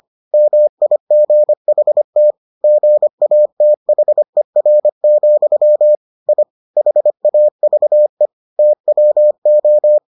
Morse code, heard as a keyed tone, about 25 wpm, 600 Hz. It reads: MIGHT GATHER, I HAVE TWO